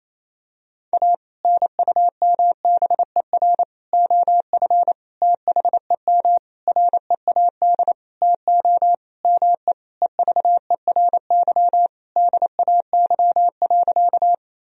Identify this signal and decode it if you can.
Morse code, heard as a keyed tone, about 28 wpm, 700 Hz. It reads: A NUMBER OF THEM READ TO ME EVERY DAY.
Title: Sentences from Top 1000 Words - Review Part 1 - 28wpm